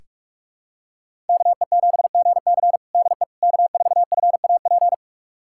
Morse code, heard as a keyed tone, about 46 wpm, 700 Hz. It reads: KE7GC DE K4FAP
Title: USA Callsign DE Callsign 45wpm